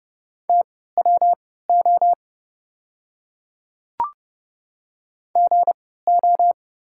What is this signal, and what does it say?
Morse code, heard as a keyed tone, about 30 wpm, spaced as for 17 wpm, 690 Hz. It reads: TWO GO
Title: 3x Character Spacing - Top 100 Words - 30wpm